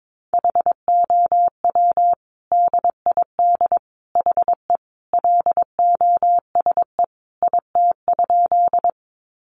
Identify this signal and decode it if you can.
Morse code, heard as a keyed tone, about 22 wpm, 700 Hz. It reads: HOW DID HE LOSE IT?
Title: Sentences from Top 1100 Words 22wpm